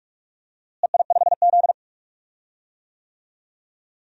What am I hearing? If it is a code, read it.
Morse code, heard as a keyed tone, about 46 wpm, 700 Hz. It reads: EI5Z
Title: DXCC-100 Call Signs - 4 Characters - Repeated 3x - 45wpm